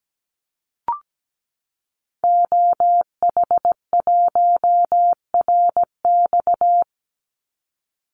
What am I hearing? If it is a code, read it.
Morse code, heard as a keyed tone, about 17 wpm, 700 Hz. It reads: OH1RX
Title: Call Signs - 17wpm